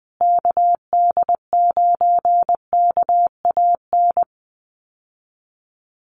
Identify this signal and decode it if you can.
Morse code, heard as a keyed tone, about 20 wpm, 700 Hz. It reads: KD9KAN
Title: USA Call Signs Repeated 3x 20wpm